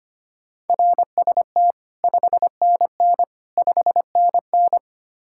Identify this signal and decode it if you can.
Morse code, heard as a keyed tone, about 32 wpm, 700 Hz. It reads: RST 5NN 5NN